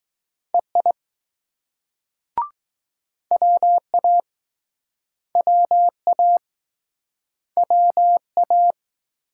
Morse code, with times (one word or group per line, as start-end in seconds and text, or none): 0.5-0.9 EI
3.3-4.2 WA
5.4-6.4 WA
7.6-8.7 WA